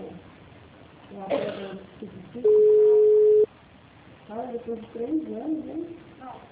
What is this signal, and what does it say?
Ringback tone (Europe, 425 Hz), with speech.
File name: out- -2023- - - .0